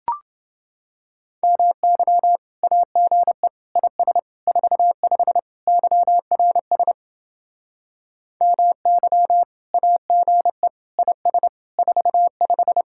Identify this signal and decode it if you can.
Morse code, heard as a keyed tone, about 28 wpm, 700 Hz. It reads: MY AGE IS 45 YRS MY AGE IS 45